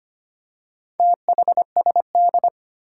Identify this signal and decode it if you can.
Morse code, heard as a keyed tone, about 25 wpm, 700 Hz. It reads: THSD
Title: Four Challenging Letter-Number Combinations 25wpm